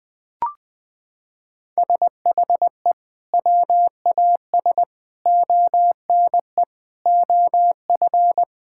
Morse code, heard as a keyed tone, about 21 wpm, 700 Hz. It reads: SHE WAS ONE OF
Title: Sentences with 4-Word Phrases - 20wpm